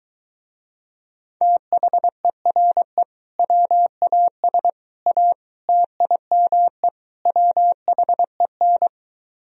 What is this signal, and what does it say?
Morse code, heard as a keyed tone, about 22 wpm, 700 Hz. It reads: THERE WAS A TIME WHEN